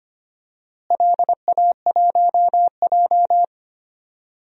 Morse code, heard as a keyed tone, about 25 wpm, 700 Hz. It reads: LA1J
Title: DXCC-100 Call Signs - 4 Characters - Repeated 3x - 25wpm